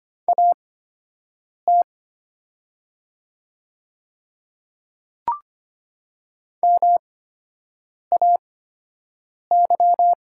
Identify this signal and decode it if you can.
Morse code, heard as a keyed tone, about 25 wpm, spaced as for 7 wpm, 700 Hz. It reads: AT MAY
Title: 8x Character Spacing - Top 100 Words - 25wpm